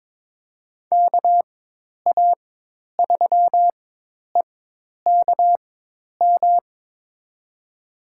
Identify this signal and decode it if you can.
Morse code, heard as a keyed tone, about 22 wpm, spaced as for 10 wpm, 700 Hz. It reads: KA3EKM
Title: USA Call Signs - 6 Characters - 4x Character Spacing - 22wpm